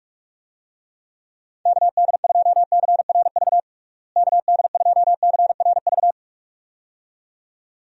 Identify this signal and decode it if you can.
Morse code, heard as a keyed tone, about 42 wpm, 700 Hz. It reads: KD2CRV KD2CRV
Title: USA Call Signs - 6 Characters - Repeated 2x - 45wpm